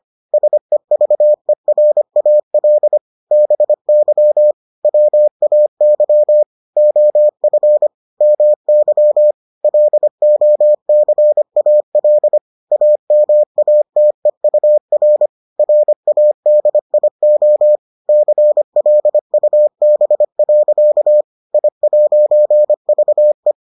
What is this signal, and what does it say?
Morse code, heard as a keyed tone, about 25 wpm, 600 Hz. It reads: SEVERAL BY WAY OF MY LOCAL AMATEUR RADIO CLUB. I'VE